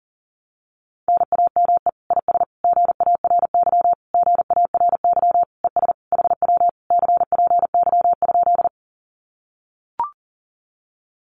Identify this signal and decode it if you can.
Morse code, heard as a keyed tone, about 40 wpm, 700 Hz. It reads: NAME IS GARY GARY ES HW CPY?
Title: Common QSO Phrases Encore 40wpm